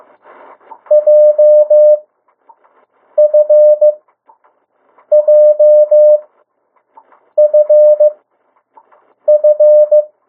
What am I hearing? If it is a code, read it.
Morse code, heard as a keyed tone, about 15 wpm, spaced as for 6 wpm, 600 Hz. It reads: JFJFF
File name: morse mp3 / 02 urok 1